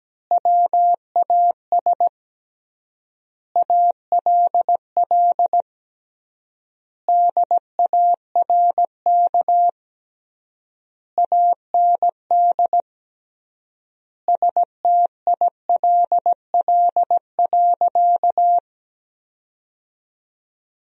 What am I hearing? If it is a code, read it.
Morse code, heard as a keyed tone, about 17 wpm, 700 Hz. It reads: WAS ALL DARK AND STILL.
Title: 3x Word-Spacing - Sentences from Top 500 Words - 17wpm